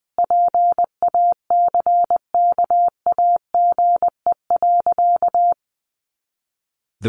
Morse code, heard as a keyed tone, about 20 wpm, 700 Hz. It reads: PACKAGE.
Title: Additional Easy Phrases 20wpm